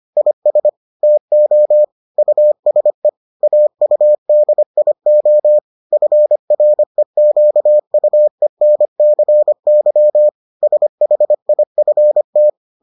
Morse code, heard as a keyed tone, about 25 wpm, 600 Hz. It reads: IS TO USE AUDIO FREQUENCY SHIFT